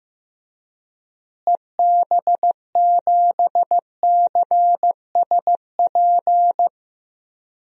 Morse code, heard as a keyed tone, about 15 wpm, 700 Hz. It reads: EB7CSP
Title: DXCC-100 Call Signs - 6 Characters - Repeated 3x - 15wpm